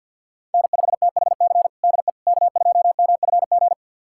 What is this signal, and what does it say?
Morse code, heard as a keyed tone, about 51 wpm, 700 Hz. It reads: N5THC DE K2MFG
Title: USA Callsign DE Callsign - 5 Characters - 50wpm